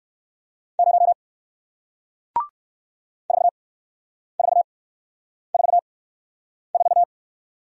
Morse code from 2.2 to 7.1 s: 4444

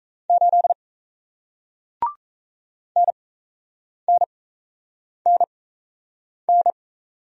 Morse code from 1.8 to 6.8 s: NNNN

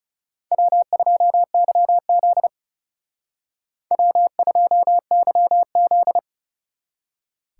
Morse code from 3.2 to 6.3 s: W2YZ

